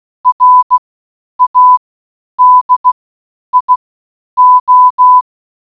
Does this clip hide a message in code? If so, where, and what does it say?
Yes: Morse code from 0.3 to 5.2 s, heard as a keyed tone, about 16 wpm, spaced as for 9 wpm, 1000 Hz. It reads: RADIO